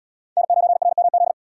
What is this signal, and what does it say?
Morse code, heard as a keyed tone, about 73 wpm, 700 Hz. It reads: N9RMZ